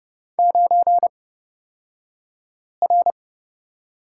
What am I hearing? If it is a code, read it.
Morse code, heard as a keyed tone, about 29 wpm, spaced as for 5 wpm, 700 Hz. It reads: 9R
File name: Lesson 36d - BT - Two Characters 30wpm